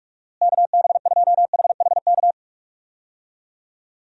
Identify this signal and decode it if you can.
Morse code, heard as a keyed tone, about 46 wpm, 700 Hz. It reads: KB2HHK
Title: USA Call Signs - 6 Characters - Repeated 2x - 45wpm